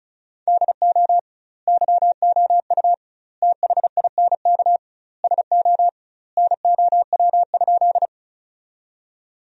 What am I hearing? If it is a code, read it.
Morse code, heard as a keyed tone, about 36 wpm, 700 Hz. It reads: DO YOU THINK SO NOW?
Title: 2x Word-Spacing - Sentences from Top 200 Words - 35wpm